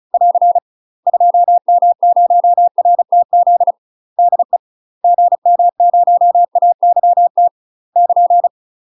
Morse code, heard as a keyed tone, about 34 wpm, 700 Hz. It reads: <AR> 2M0RTZ DE GM0AYT <KN>
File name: UK MASTER QSOs TRAINING / 35WPM / UK MASTER QSOs 0002